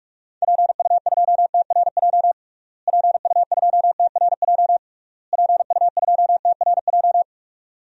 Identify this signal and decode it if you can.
Morse code, heard as a keyed tone, about 44 wpm, 700 Hz. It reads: PU2TRJ PU2TRJ PU2TRJ
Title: DXCC-100 Call Signs - 6 Characters - Repeated 3x - 45wpm